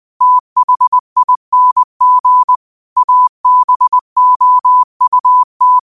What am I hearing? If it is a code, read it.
Morse code, heard as a keyed tone, about 20 wpm, 1000 Hz. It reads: THING ABOUT